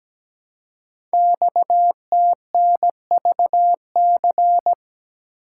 Morse code, heard as a keyed tone, about 17 wpm, 700 Hz. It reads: XTNVC
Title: Code Groups - 5 Letter:Numbers - 17wpm